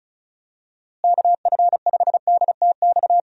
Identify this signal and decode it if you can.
Morse code, heard as a keyed tone, about 34 wpm, 700 Hz. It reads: KF5DTX
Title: USA Call Signs - 6 Characters - Repeated 3x - 35wpm